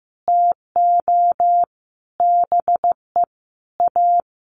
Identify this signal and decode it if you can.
Morse code, heard as a keyed tone, about 15 wpm, 700 Hz. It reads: TO BE A
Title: Sets of 5 Words Encore 15wpm